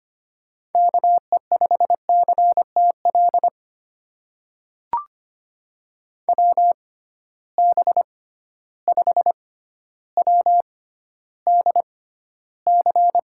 Morse code, heard as a keyed tone, about 25 wpm, spaced as for 13 wpm, 700 Hz. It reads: KE5CTL WB5WDC
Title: USA Call Signs - 6 Characters - 6x Character Spacing - 25wpm